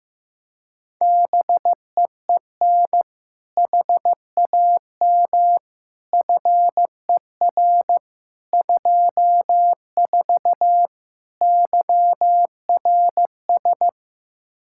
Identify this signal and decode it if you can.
Morse code, heard as a keyed tone, about 15 wpm, 700 Hz. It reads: BEEN HAM FER 24 YRS
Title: Practice QSO - Exchange 3 - More Info - 15wpm